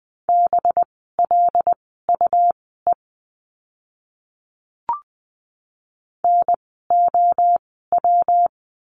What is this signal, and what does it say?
Morse code, heard as a keyed tone, about 20 wpm, spaced as for 14 wpm, 700 Hz. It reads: BLUE NOW
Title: 185 Lesson 30f - Letter b - Words 20wpm